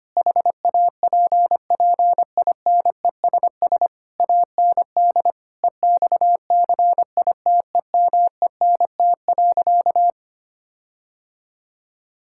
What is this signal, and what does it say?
Morse code, heard as a keyed tone, about 25 wpm, 700 Hz. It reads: HAPPINESS AND EXCITEMENT.
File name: practice / 25wpm / Difficult Sentences with Hints - 25wpm